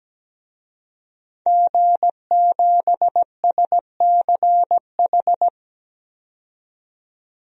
Morse code, heard as a keyed tone, about 17 wpm, 700 Hz. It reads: G7SCH
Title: DXCC-100 Call Signs - 5 Characters - 17wpm